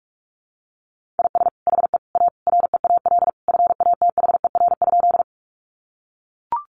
Morse code, heard as a keyed tone, about 44 wpm, 700 Hz. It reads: IS HE A REAL FATHER?
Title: Sentences from Top 300 Words 45wpm